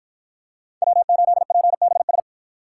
Morse code, heard as a keyed tone, about 53 wpm, 700 Hz. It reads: W8PBS